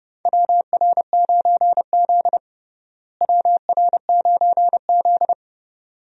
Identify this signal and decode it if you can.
Morse code, heard as a keyed tone, about 30 wpm, 700 Hz. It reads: WR9Z WR9Z
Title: USA Call Signs - 4 Characters - Repeated 2x - 30wpm